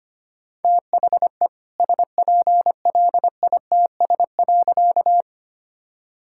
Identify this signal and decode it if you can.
Morse code, heard as a keyed tone, about 25 wpm, 700 Hz. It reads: THE SPLITS.
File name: practice / 25wpm / All-Age Appropriate Jokes - 25wpm